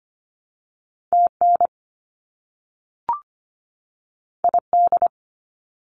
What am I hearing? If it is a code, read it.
Morse code, heard as a keyed tone, about 25 wpm, 700 Hz. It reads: TN ID
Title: US States 25wpm